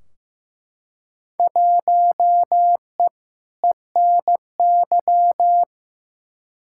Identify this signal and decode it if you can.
Morse code, heard as a keyed tone, about 15 wpm, 700 Hz. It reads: 1E ENY
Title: Field Day Exchanges - 15wpm